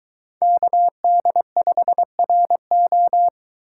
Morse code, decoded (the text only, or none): KD5RO